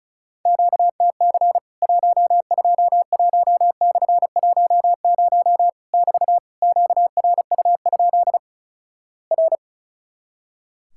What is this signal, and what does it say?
Morse code, heard as a keyed tone, about 36 wpm, 700 Hz. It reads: QTC 121/10 <BT> QRU? R
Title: Worked All Europe DX Contest - QTC Exchange - 35wpm